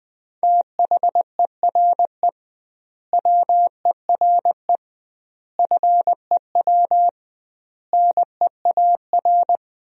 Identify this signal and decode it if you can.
Morse code, heard as a keyed tone, about 21 wpm, 700 Hz. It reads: THERE WERE FEW NEAR
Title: 2x Word-Spacing - Sentences from Top 300 Words - 20wpm